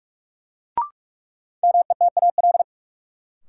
Morse code, heard as a keyed tone, about 43 wpm, 700 Hz. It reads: METAL
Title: Sentences from Top 1600 Words - New Word Drill Alt - 45wpm